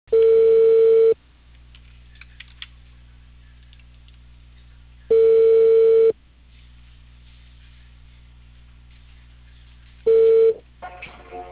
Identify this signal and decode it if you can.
Ringback tone (China, 450 Hz).